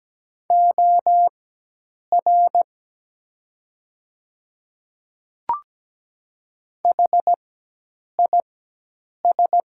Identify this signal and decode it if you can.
Morse code, heard as a keyed tone, about 17 wpm, spaced as for 8 wpm, 700 Hz. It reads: OR HIS
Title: 4x Character Spacing - Top 100 Words - 17wpm